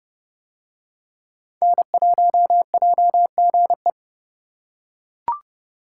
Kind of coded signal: Morse code

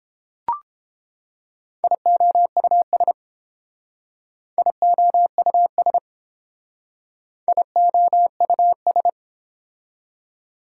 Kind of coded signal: Morse code

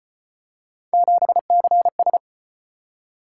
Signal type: Morse code

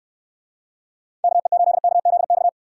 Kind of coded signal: Morse code

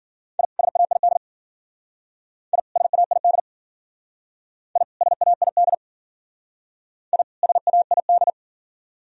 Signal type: Morse code

